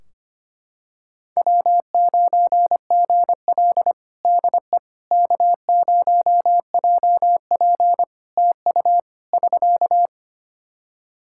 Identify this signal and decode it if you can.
Morse code, heard as a keyed tone, about 25 wpm, 700 Hz. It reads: W9GL DE K0JP TU <SK>
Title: Practice QSO - Exchange 4 - Ending - 25wpm